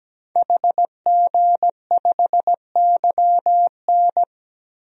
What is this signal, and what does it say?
Morse code, heard as a keyed tone, about 17 wpm, 700 Hz. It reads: HG5YN
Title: DXCC-100 Call Signs - 5 Characters - 17wpm